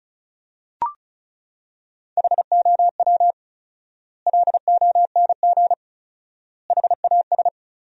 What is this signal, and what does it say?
Morse code, heard as a keyed tone, about 36 wpm, 700 Hz. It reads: HOW LONG HAS
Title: 4x Word-Spacing - Sentences from Top 100 Words - 35wpm